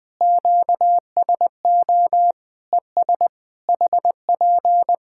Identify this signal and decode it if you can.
Morse code, heard as a keyed tone, about 20 wpm, 700 Hz. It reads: QSO ES HP
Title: Practice QSO - Exchange 4 - Ending - 20wpm